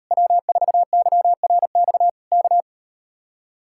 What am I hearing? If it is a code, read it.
Morse code, heard as a keyed tone, about 39 wpm, 700 Hz. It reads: W4YRX K